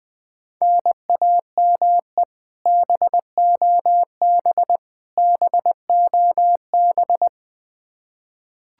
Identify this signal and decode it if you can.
Morse code, heard as a keyed tone, about 21 wpm, 700 Hz. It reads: NAME BOB BOB
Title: Practice QSO - Exchange 1 - Beginning - 20wpm